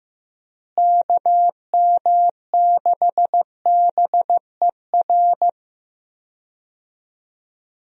Morse code, heard as a keyed tone, about 15 wpm, 700 Hz. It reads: KM6BER
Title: USA Call Signs - 6 Characters - Repeated 3x - 15wpm